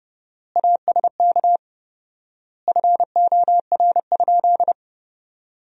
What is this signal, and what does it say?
Morse code, heard as a keyed tone, about 28 wpm, 700 Hz. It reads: ASK FOR?
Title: 4x Word-Spacing - Sentences from Top 200 Words - 30wpm